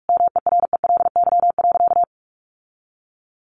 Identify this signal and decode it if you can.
Morse code, heard as a keyed tone, about 44 wpm, 700 Hz. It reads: MERELY.
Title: Sentences from Top 1400 Words 45wpm